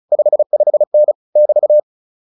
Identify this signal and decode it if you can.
Morse code, heard as a keyed tone, about 34 wpm, 600 Hz. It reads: 55N <BT>